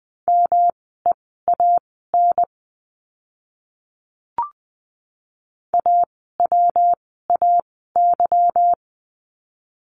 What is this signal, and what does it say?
Morse code, heard as a keyed tone, about 20 wpm, spaced as for 15 wpm, 700 Hz. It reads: MEAN AWAY
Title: 146 Lesson 24f - Letter y - Words 20wpm